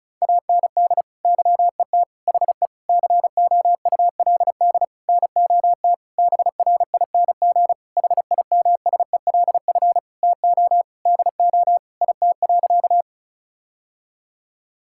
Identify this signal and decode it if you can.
Morse code, heard as a keyed tone, about 36 wpm, 700 Hz. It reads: AND YET HE COULD NOT BRING HIMSELF TO DO IT.
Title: Sentences from Top 1600 Words 35wpm